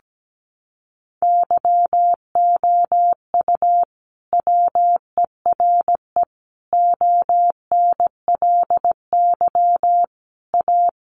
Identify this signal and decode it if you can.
Morse code, heard as a keyed tone, about 17 wpm, 700 Hz. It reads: YOU WERE ONLY A
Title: Sentences from Top 1000 Words 17wpm